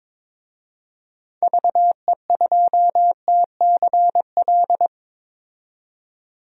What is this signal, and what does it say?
Morse code, heard as a keyed tone, about 22 wpm, 700 Hz. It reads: VE2TCL
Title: DXCC-100 Call Signs - 6 Characters - 22wpm